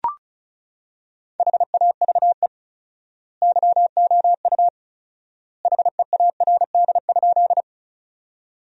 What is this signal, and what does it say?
Morse code, heard as a keyed tone, about 34 wpm, 700 Hz. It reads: HAVE YOU HEARD?
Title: 4x Word-Spacing - Sentences from Top 400 Words - 35wpm